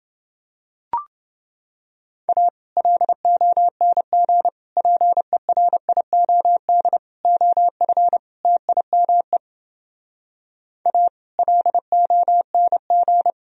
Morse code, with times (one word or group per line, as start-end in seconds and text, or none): 2.3-2.5 A
2.8-4.5 LONG
4.8-7.0 PERIOD
7.2-8.2 OF
8.4-9.4 TIME
10.9-11.1 A
11.4-13.3 LONG